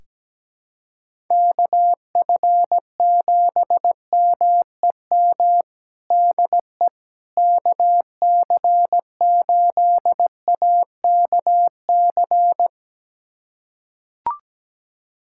Morse code, heard as a keyed tone, about 17 wpm, 700 Hz. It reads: KF7MEM DE KC8AKC
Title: USA Callsign DE Callsign - 6 Characters - 17wpm